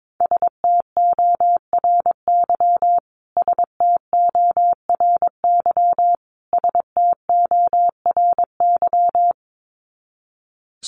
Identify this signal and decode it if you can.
Morse code, heard as a keyed tone, about 22 wpm, 700 Hz. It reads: STORY STORY STORY
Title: Top 500 Words Repeated 22wpm